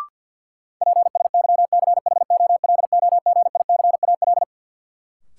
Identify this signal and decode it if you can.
Morse code, heard as a keyed tone, about 53 wpm, 700 Hz. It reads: PSYCHOLOGICAL
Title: Sentences from Top 1400 Words - New Word Drill Alt - 50wpm